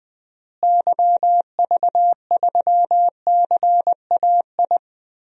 Morse code, heard as a keyed tone, about 20 wpm, 700 Hz. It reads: YV3CAI
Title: DXCC-100 Call Signs - 6 Characters - Repeated 3x - 20wpm